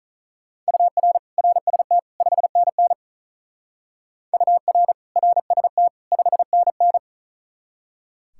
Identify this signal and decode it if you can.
Morse code, heard as a keyed tone, about 45 wpm, 700 Hz. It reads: UR RST 5NN UR RST 5NN